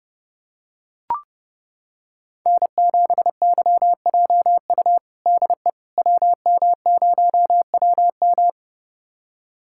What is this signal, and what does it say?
Morse code, heard as a keyed tone, about 30 wpm, 700 Hz. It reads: N7YJU DE WM0WM
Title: USA Callsign DE Callsign - 5 Characters - 30wpm